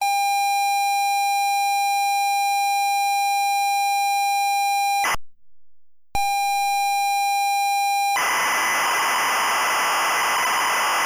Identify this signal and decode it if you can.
ZX Spectrum tape loading.
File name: Forums / SpectrumComputing-BlackMirror-Bandersnatch-ZXSpectrumTape